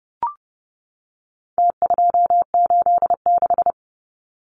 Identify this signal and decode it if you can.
Morse code, heard as a keyed tone, about 30 wpm, 700 Hz. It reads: T286
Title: Numbers - Contest Style - 4-Digit 30wpm